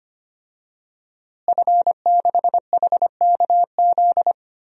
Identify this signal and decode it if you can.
Morse code, heard as a keyed tone, about 25 wpm, 700 Hz. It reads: F6HKZ
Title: DXCC-100 Call Signs - 5 Characters - Repeated 2x - 25wpm